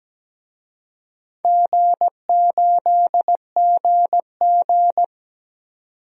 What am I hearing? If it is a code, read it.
Morse code, heard as a keyed tone, about 17 wpm, 700 Hz. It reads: G8GG
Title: DXCC-100 Call Signs - 4 Characters - 17wpm